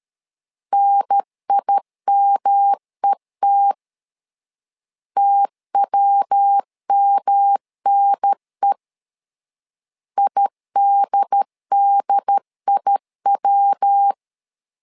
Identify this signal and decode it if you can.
Morse code, heard as a keyed tone, about 13 wpm, 800 Hz. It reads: NIMET TWMNE IDDIW
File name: LEZIONI / 3 MWRO / 3 MW 12wpm 1633